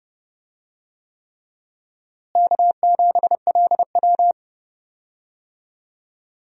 Morse code, heard as a keyed tone, about 31 wpm, 700 Hz. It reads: K7LW
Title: USA Call Signs - 4 Characters - Repeated 3x - 30wpm